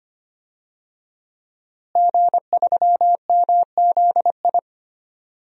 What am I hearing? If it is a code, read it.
Morse code, heard as a keyed tone, about 25 wpm, 700 Hz. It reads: G3MZI
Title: DXCC-100 Call Signs - 5 Characters - Repeated 3x - 25wpm